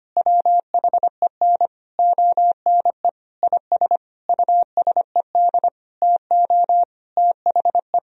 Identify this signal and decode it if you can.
Morse code, heard as a keyed tone, about 25 wpm, 700 Hz. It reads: WHEN ONE IS USED TO THE
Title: Sentences from Top 1000 Words - Review Part 1 - 25wpm